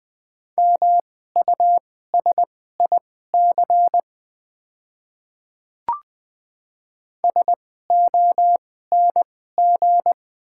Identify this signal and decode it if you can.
Morse code, heard as a keyed tone, about 20 wpm, spaced as for 14 wpm, 700 Hz. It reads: MUSIC SONG
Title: 2x Character Spacing - Top 500 Words - 20wpm